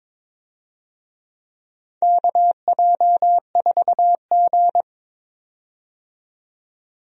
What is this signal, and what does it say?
Morse code, heard as a keyed tone, about 22 wpm, 700 Hz. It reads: KJ4G
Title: USA Call Signs - 4 Characters - 22wpm